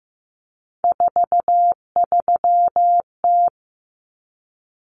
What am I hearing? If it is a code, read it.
Morse code, heard as a keyed tone, about 15 wpm, 700 Hz. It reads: 43T
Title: Common Cut Numbers - 3-Digit 15wpm